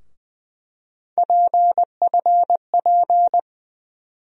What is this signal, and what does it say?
Morse code, heard as a keyed tone, about 20 wpm, 700 Hz. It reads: PFP
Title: USA Call Sign Postfixes - 3 Letters 20wpm